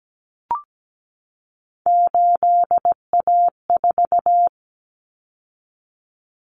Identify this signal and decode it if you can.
Morse code, heard as a keyed tone, about 17 wpm, 700 Hz. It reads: 8A4
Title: Common Cut Numbers - 3-Digit 17wpm